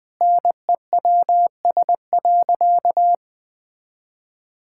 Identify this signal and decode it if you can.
Morse code, heard as a keyed tone, about 20 wpm, 700 Hz. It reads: NEWS.
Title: Sentences with 3-Word Phrases - 20wpm